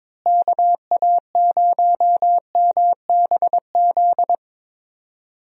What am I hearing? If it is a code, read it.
Morse code, heard as a keyed tone, about 21 wpm, 700 Hz. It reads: KA0MBZ
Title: USA Call Signs - 6 Characters - Repeated 2x - 22wpm